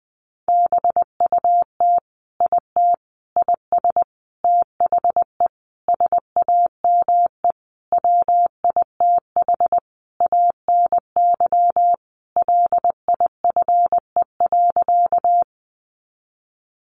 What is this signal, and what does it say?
Morse code, heard as a keyed tone, about 20 wpm, 700 Hz. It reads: BUT IT IS THE SAME WITH ANY LIFE.
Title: Sentences from Top 300 Words 20wpm